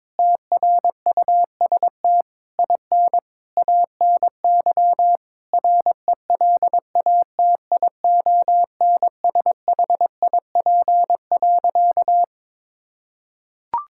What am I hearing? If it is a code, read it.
Morse code, heard as a keyed tone, about 22 wpm, 700 Hz. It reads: TRUST IN ANY RELATIONSHIP.
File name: Difficult Sentences with Hints - 22wpm